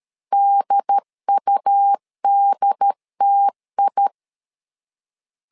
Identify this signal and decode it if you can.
Morse code, heard as a keyed tone, about 13 wpm, 800 Hz. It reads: DUDTI
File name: LEZIONI / 4 Q 12wpm 1833